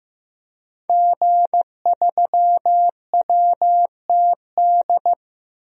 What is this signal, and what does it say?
Morse code, heard as a keyed tone, about 15 wpm, 700 Hz. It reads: G3WTD
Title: DXCC-100 Call Signs - 5 Characters - 15wpm